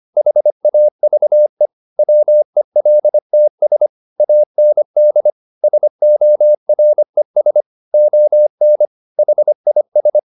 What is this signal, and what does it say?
Morse code, heard as a keyed tone, about 25 wpm, 600 Hz. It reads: HAVE WELTS AND SORES ON HIS
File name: the old man and the sea part 1 25wpm